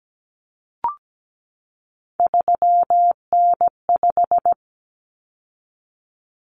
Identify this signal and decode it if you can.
Morse code, heard as a keyed tone, about 17 wpm, 700 Hz. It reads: 3N5
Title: Common Cut Numbers - 3-Digit 17wpm